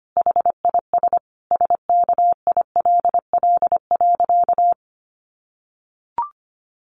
Morse code, heard as a keyed tone, about 25 wpm, 700 Hz. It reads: HIS SKILL.
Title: Sentences from Top 1200 Words 25wpm